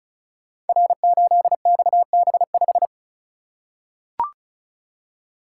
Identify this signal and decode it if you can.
Morse code, heard as a keyed tone, about 36 wpm, 700 Hz. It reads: R8XB5
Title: Code Groups - 5 Letter:Numbers - 35wpm